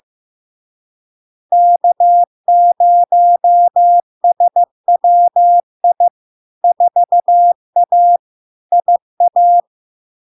Morse code, heard as a keyed tone, about 15 wpm, 700 Hz. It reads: K0SWI 4A IA
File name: LARK-FD-2022-Log-in-CW-15wpm-700-Hz